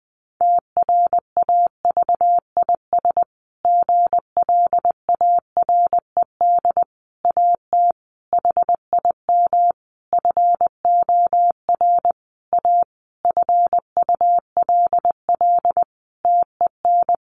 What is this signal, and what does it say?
Morse code, heard as a keyed tone, about 20 wpm, 700 Hz. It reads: TRAVIS GLARED AT HIM FOR A FULL TEN
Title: A Sound Of Thunder 20wpm